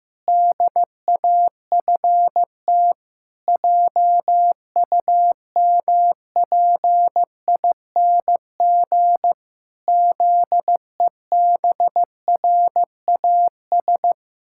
Morse code, heard as a keyed tone, about 15 wpm, 700 Hz. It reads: DAFT JUMPING ZEBRAS
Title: Pangram Challenge v2 - 15wpm